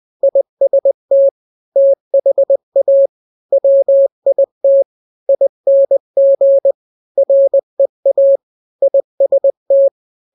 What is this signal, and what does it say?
Morse code, heard as a keyed tone, about 20 wpm, 550 Hz. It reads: IST THA WIT ING REA IST
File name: n-grams / 3-grams 20 18 WPM